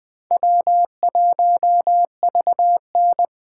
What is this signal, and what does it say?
Morse code, heard as a keyed tone, about 21 wpm, 700 Hz. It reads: W1VN
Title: USA Call Signs - 4 Characters - 2x Character Spacing - 20wpm